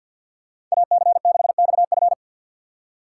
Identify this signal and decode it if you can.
Morse code, heard as a keyed tone, about 53 wpm, 700 Hz. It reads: AC6XF
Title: USA Call Signs - 5 Characters - 3x Character Spacing - 50wpm